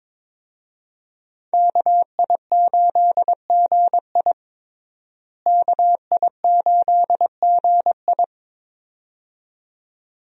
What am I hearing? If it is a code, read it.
Morse code, heard as a keyed tone, about 22 wpm, 700 Hz. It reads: KI8GI KI8GI
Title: USA Call Signs - 5 Characters - Repeated 2x - 22wpm